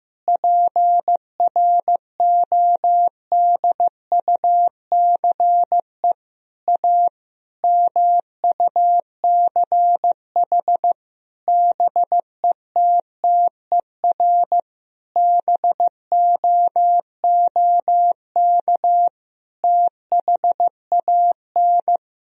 Morse code, heard as a keyed tone, about 15 wpm, 700 Hz. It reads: PRODUCE A MUCH BETTER BOOK THAN
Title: Sentences from Top 1000 Words - Review Part 3 - 15wpm